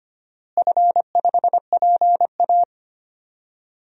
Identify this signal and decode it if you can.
Morse code, heard as a keyed tone, about 25 wpm, 700 Hz. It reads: F5PA
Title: DXCC-100 Call Signs - 4 Characters - Repeated 2x - 25wpm